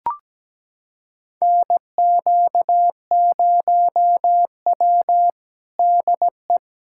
Morse code, heard as a keyed tone, about 17 wpm, 700 Hz. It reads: NQ0W DE
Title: USA Callsign DE Callsign 17wpm